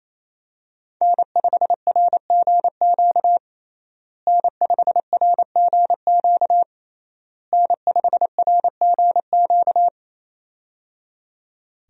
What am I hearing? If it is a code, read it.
Morse code, heard as a keyed tone, about 28 wpm, 700 Hz. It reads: N5RGQ N5RGQ N5RGQ
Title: USA Call Signs - 5 Characters - Repeated 3x - 28wpm